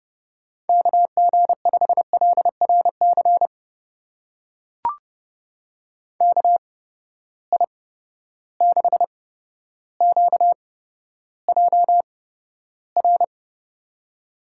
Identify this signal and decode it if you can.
Morse code, heard as a keyed tone, about 30 wpm, spaced as for 13 wpm, 700 Hz. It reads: KG5LRC KI6QJR
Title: USA Call Signs - 6 Characters - 8x Character Spacing - 30wpm